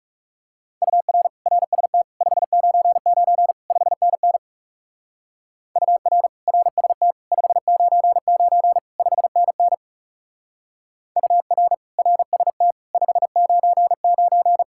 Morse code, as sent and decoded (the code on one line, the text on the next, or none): ..- .-. / .-. ... - / ..... ----. ----. / ..... -. -. / ..- .-. / .-. ... - / ..... ----. ----. / ..... -. -. / ..- .-. / .-. ... - / ..... ----. ----.
UR RST 599 5NN UR RST 599 5NN UR RST 599